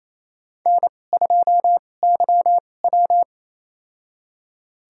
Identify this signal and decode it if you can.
Morse code, heard as a keyed tone, about 28 wpm, spaced as for 20 wpm, 700 Hz. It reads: N2YW
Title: USA Call Signs - 2x Character Spacing - 28wpm